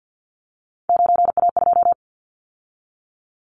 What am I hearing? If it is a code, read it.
Morse code, heard as a keyed tone, about 49 wpm, 700 Hz. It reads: 9A2
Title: Callsign Prefixes for Top 25 DXCC - 50wpm